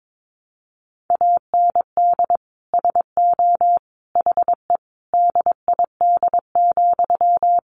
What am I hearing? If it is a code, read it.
Morse code, heard as a keyed tone, about 22 wpm, 700 Hz. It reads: AND SO HE DID,
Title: Sentences from Top 1500 Words 22wpm